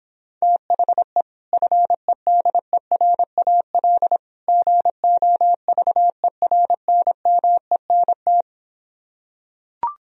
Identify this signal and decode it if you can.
Morse code, heard as a keyed tone, about 26 wpm, 700 Hz. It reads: THE FEDERAL GOVERNMENT